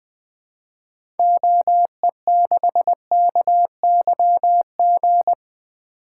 Morse code, heard as a keyed tone, about 20 wpm, 700 Hz. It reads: OE6KYG
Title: 2023 SOTA Activator Call Signs - 20wpm